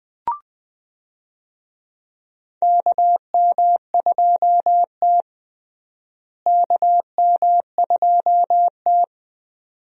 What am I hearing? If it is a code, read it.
Morse code, heard as a keyed tone, about 20 wpm, 700 Hz. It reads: KM2T KM2T
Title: USA Call Signs - 4 Characters - Repeated 2x - 20wpm